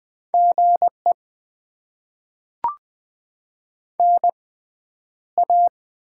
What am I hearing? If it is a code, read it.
Morse code, heard as a keyed tone, about 20 wpm, 700 Hz. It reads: G E N A
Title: Top 20 QSO Elements - 6x Character Spacing - 20wpm